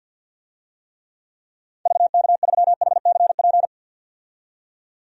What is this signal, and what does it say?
Morse code, heard as a keyed tone, about 49 wpm, 700 Hz. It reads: VK3HCP
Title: DXCC-100 Call Signs - 6 Characters - 50wpm